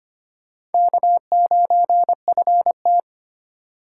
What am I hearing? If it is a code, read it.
Morse code, heard as a keyed tone, about 34 wpm, 700 Hz. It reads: K9FT